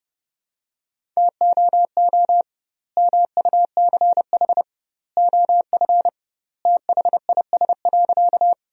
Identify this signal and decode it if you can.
Morse code, heard as a keyed tone, about 32 wpm, 700 Hz. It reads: TOO MUCH OF THIS.
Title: 2x Word-Spacing - Sentences from Top 200 Words - 30wpm